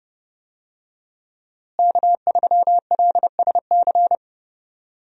Morse code, heard as a keyed tone, about 30 wpm, 700 Hz. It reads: K3LSC
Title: USA Call Signs - 5 Characters - Repeated 3x - 30wpm